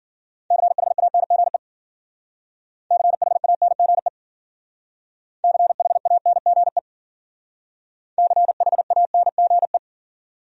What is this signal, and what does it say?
Morse code, heard as a keyed tone, about 52 wpm, 700 Hz. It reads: CHANGE CHANGE CHANGE CHANGE